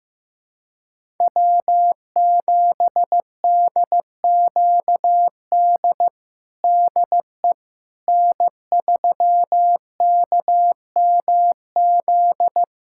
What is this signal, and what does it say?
Morse code, heard as a keyed tone, about 15 wpm, 700 Hz. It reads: W7DQD DE N3KMZ
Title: USA Callsign DE Callsign - 5 Characters - 15wpm